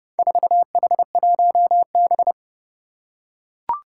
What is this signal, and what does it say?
Morse code, heard as a keyed tone, about 32 wpm, 700 Hz. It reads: 4H1B
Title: Four Challenging Letter-Number Combinations 30wpm